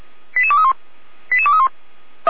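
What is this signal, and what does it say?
Five-tone selective call (ZVEI), heard as quick runs of five tones, tones of about 80 ms. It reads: 88231 88231
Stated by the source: ZVEI five-tone selective call received by SDR.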